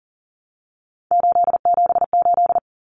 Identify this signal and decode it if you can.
Morse code, heard as a keyed tone, about 40 wpm, 700 Hz. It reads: 878
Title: Common Cut Numbers - 3-Digit 40wpm